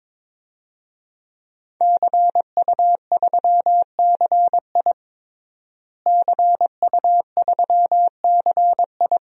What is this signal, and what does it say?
Morse code, heard as a keyed tone, about 22 wpm, 700 Hz. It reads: CU3CI CU3CI
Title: DXCC-100 Call Signs - 5 Characters - Repeated 2x - 22wpm